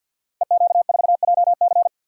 Morse code, heard as a keyed tone, about 52 wpm, 700 Hz. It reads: EQ4JC
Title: Code Groups - 5 Letter:Numbers - 50wpm